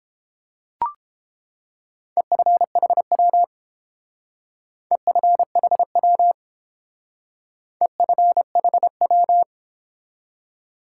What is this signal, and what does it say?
Morse code, heard as a keyed tone, about 29 wpm, 700 Hz. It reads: EFHW EFHW EFHW